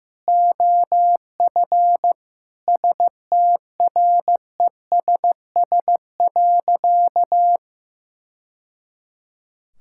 Morse code, heard as a keyed tone, about 15 wpm, 700 Hz. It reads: OF STRESS.
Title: Sentences from Top 1000 Words - Review Part 6 - 15wpm